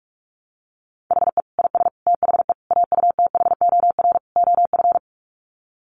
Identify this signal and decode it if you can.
Morse code, heard as a keyed tone, about 44 wpm, 700 Hz. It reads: HE IS THE AUTHOR OF
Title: Sets of 5 Words Encore2 45wpm